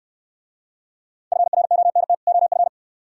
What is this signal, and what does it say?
Morse code, heard as a keyed tone, about 65 wpm, 690 Hz. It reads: FRONT OF